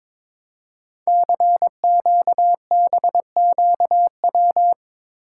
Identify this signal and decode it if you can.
Morse code, heard as a keyed tone, about 22 wpm, 700 Hz. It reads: CQBQW
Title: Code Groups - 5 Letter:Numbers - 22wpm